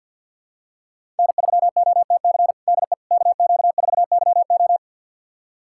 Morse code, heard as a keyed tone, about 51 wpm, 700 Hz. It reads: N3OTC DE KQ4YO
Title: USA Callsign DE Callsign - 5 Characters - 50wpm